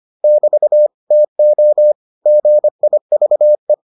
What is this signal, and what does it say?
Morse code, heard as a keyed tone, about 25 wpm, 600 Hz. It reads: <BT> TO GIVE